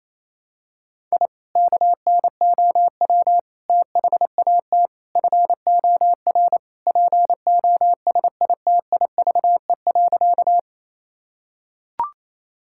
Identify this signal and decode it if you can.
Morse code, heard as a keyed tone, about 28 wpm, 700 Hz. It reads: I KNOW THAT FOR POSITIVE.
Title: Sentences from Top 1000 Words - Review Part 5 - 28wpm